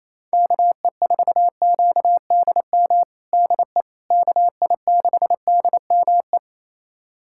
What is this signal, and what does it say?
Morse code, heard as a keyed tone, about 28 wpm, 700 Hz. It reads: KE4QDM DE KI6DME
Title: USA Callsign DE Callsign - 6 Characters - 28wpm